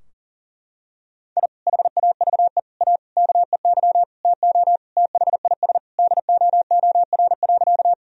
Morse code, heard as a keyed tone, about 40 wpm, 700 Hz. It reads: I HAVE A KEY TO THIS DOOR.
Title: Sentences from Top 1000 Words - Review Part 4 - 40wpm